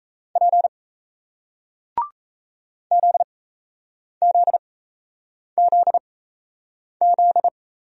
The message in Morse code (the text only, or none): P ZZZZ